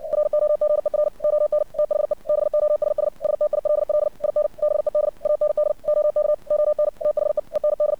Morse code, heard as a keyed tone, about 61 wpm, 600 Hz. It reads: ROGER ON THE BOSS STILL IN BED AND OK ON THE END
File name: archived / CRSnet